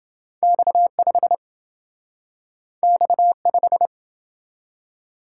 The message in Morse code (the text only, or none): X5 X5